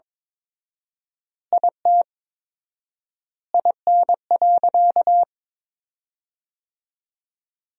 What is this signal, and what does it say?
Morse code, heard as a keyed tone, about 22 wpm, 700 Hz. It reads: IT IN.
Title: 4x Word-Spacing - Sentences from Top 200 Words - 22wpm